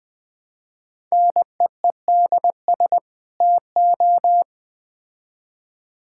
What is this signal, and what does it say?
Morse code, heard as a keyed tone, about 20 wpm, 700 Hz. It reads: NEEDS TO
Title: Build A Phrase - 3-Word Phrases - 20wpm